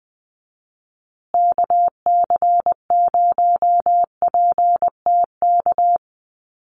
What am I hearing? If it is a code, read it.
Morse code, heard as a keyed tone, about 20 wpm, 700 Hz. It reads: KC0PTK
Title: USA Call Signs Repeated 2x 20wpm